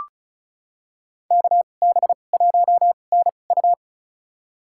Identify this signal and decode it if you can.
Morse code, heard as a keyed tone, about 34 wpm, spaced as for 25 wpm, 700 Hz. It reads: KB1NU
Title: USA Call Signs - 2x Character Spacing - 35wpm